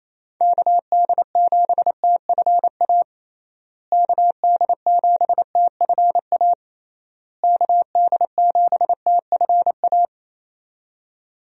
Morse code, heard as a keyed tone, about 28 wpm, 700 Hz. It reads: KD7TFA KD7TFA KD7TFA
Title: USA Call Signs - 6 Characters - Repeated 3x - 28wpm